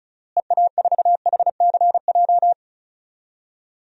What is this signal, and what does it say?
Morse code, heard as a keyed tone, about 34 wpm, 700 Hz. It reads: EA4HCJ
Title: DXCC-100 Call Signs - 6 Characters - Repeated 2x - 35wpm